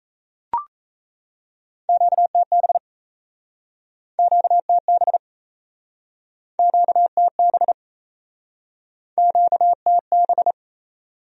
Morse code, heard as a keyed tone, about 37 wpm, 700 Hz. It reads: QTB QTB QTB QTB